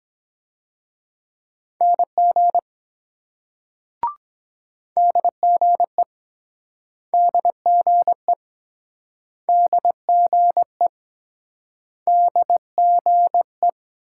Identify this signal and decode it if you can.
Morse code, heard as a keyed tone, about 23 wpm, 700 Hz. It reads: NG DGE DGE DGE DGE